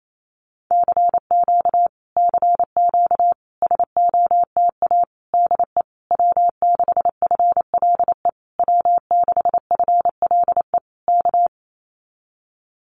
Morse code, heard as a keyed tone, about 28 wpm, 700 Hz. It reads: CQ CQ SOTA DE W6FLE W6FLE K
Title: Common QSO Phrases Encore 28wpm